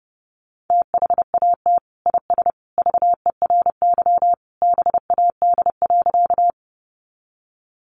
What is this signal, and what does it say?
Morse code, heard as a keyed tone, about 30 wpm, 700 Hz. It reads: THAT IS VERY BAD.
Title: Sentences from Top 700 Words 30wpm